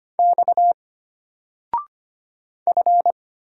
Morse code, heard as a keyed tone, about 25 wpm, 700 Hz. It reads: X F
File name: Single-Letter-Number-25wpm